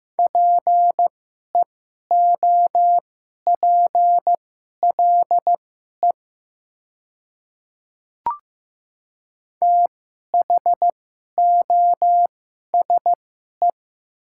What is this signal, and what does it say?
Morse code, heard as a keyed tone, about 15 wpm, spaced as for 11 wpm, 700 Hz. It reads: PEOPLE THOSE
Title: 2x Character Spacing - Top 300 Words - 15wpm